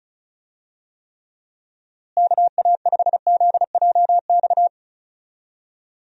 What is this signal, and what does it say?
Morse code, heard as a keyed tone, about 34 wpm, 700 Hz. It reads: KA5ZJX
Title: USA Call Signs - 6 Characters - Repeated 3x - 35wpm